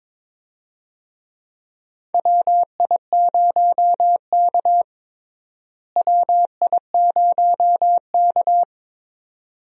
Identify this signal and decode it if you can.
Morse code, heard as a keyed tone, about 22 wpm, 700 Hz. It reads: WI0K WI0K
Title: USA Call Signs - 4 Characters - Repeated 2x - 22wpm